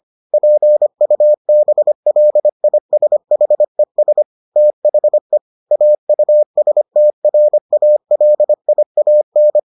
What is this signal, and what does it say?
Morse code, heard as a keyed tone, about 25 wpm, 600 Hz. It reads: PUBLISHES THE AUSTRALIAN